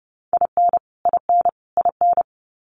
Morse code, heard as a keyed tone, about 30 wpm, 700 Hz. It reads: IN IN IN
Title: US States Repeated 3x 30wpm